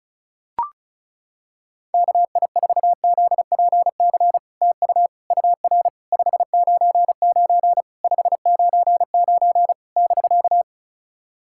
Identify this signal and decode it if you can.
Morse code, heard as a keyed tone, about 36 wpm, 700 Hz. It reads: KI4ZPC TU UR 599 599 <BK>
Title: SOTA QSO Practice 35wpm